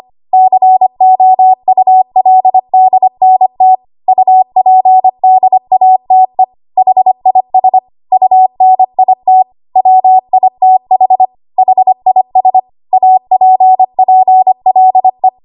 Morse code, heard as a keyed tone, about 25 wpm, 750 Hz. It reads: COULDNT UPDATE HIS UNIT WITH HIS APPLE
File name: Archive / 25 WPM / 25WPM